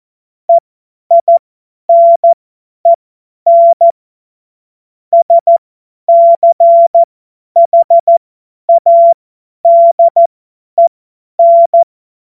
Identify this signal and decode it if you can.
Morse code, heard as a keyed tone, about 14 wpm, spaced as for 10 wpm, 680 Hz. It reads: EINEN SCHADEN
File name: Morsekurs / Witze / Witz 13 Februar 2024 50 F70